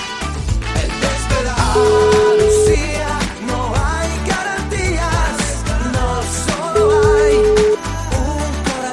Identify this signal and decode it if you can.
Ringback tone (Europe, 425 Hz), with noise.